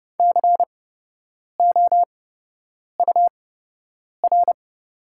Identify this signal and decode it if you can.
Morse code, heard as a keyed tone, about 30 wpm, spaced as for 8 wpm, 700 Hz. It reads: COUR
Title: 8x Character Spacing - Top 500 Words - 30wpm